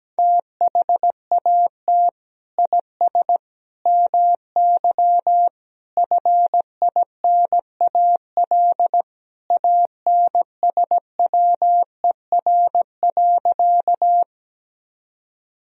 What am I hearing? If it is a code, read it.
Morse code, heard as a keyed tone, about 17 wpm, 700 Hz. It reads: THAT IS MY FINAL ANSWER.
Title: Sentences from Top 1000 Words - Review Part 2 - 17wpm